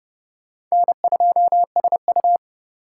Morse code, heard as a keyed tone, about 36 wpm, 700 Hz. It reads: N2SU
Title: Speed Racing - USA Call Signs 30wpm